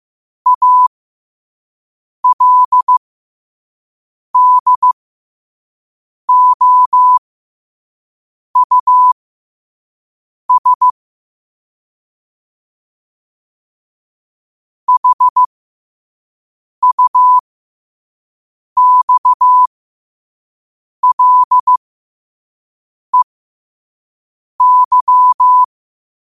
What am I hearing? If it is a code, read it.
Morse code, heard as a keyed tone, about 15 wpm, spaced as for 5 wpm, 1000 Hz. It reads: ALDOUS HUXLEY
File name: qotdfiles / 5wpm